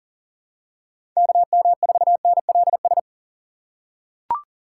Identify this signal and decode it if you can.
Morse code, heard as a keyed tone, about 40 wpm, 700 Hz. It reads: KM4NLS
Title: USA Call Signs - 6 Characters - 3x Character Spacing - 40wpm